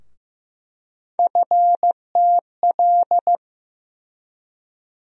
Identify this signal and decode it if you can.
Morse code, heard as a keyed tone, about 15 wpm, 700 Hz. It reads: FTL
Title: USA Call Sign Postfixes - 3 Letters 15wpm